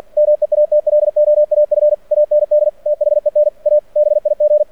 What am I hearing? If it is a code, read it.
Morse code, heard as a keyed tone, about 48 wpm, 600 Hz. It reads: MEATLOAF AND THEN A BIG
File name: archived / CRSnet